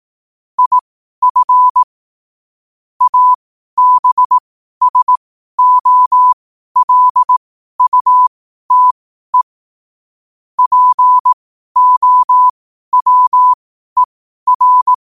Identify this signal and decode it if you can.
Morse code, heard as a keyed tone, about 18 wpm, spaced as for 12 wpm, 1000 Hz. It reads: IF ABSOLUTE POWER
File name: qotdfiles / 12wpm